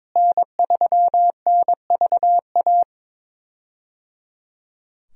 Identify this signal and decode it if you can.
Morse code, heard as a keyed tone, about 22 wpm, 700 Hz. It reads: N3NVA
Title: USA Call Signs - 5 Characters - Repeated 3x - 22wpm